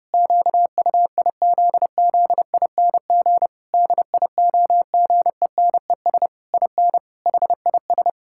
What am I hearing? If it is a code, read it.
Morse code, heard as a keyed tone, about 30 wpm, 700 Hz. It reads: QUIZZING DIOGENES IN HIS
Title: Pangram Challenge v2 - 30wpm